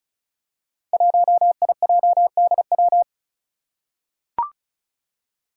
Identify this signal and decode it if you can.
Morse code, heard as a keyed tone, about 34 wpm, 700 Hz. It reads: 1IJDW
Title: Five Challenging Letter-Number Combinations 35wpm